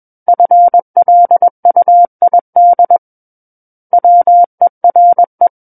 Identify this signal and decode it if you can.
Morse code, heard as a keyed tone, about 21 wpm, 700 Hz. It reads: FLUID WERE
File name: Book One / 12-Chapter 12 - 21 WPM